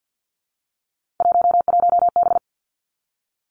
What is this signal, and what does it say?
Morse code, heard as a keyed tone, about 50 wpm, 700 Hz. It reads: 11B
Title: Three Challenging Letter-Number Combinations 50wpm